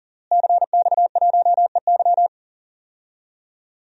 Morse code, heard as a keyed tone, about 40 wpm, 700 Hz. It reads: CX1EY
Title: DXCC-100 Call Signs - 5 Characters - Repeated 3x - 40wpm